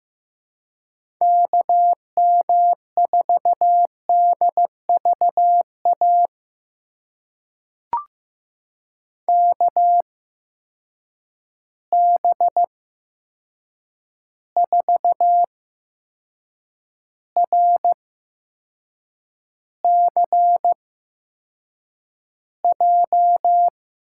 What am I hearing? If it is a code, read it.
Morse code, heard as a keyed tone, about 15 wpm, spaced as for 6 wpm, 700 Hz. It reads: KM4DVA KB4RCJ
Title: USA Call Signs - 6 Characters - 8x Character Spacing - 15wpm